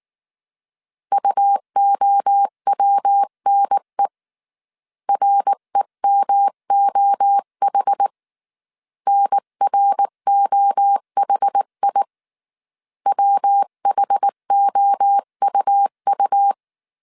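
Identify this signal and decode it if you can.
Morse code, heard as a keyed tone, about 19 wpm, 800 Hz. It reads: UOWNE REMOH NROHI WHOUU
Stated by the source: Morse code dictation at 18 wpm from the IW3SGT QRS Morse code training course, five-character groups of 300 characters.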